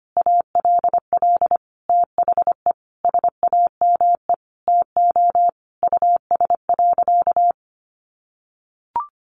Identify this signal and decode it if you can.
Morse code, heard as a keyed tone, about 25 wpm, 700 Hz. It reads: ALL THE SAME TO US.
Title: Sentences from Top 600 Words 25wpm